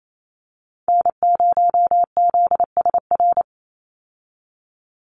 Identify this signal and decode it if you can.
Morse code, heard as a keyed tone, about 28 wpm, 700 Hz. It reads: N0ZSR
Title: USA Call Signs 28wpm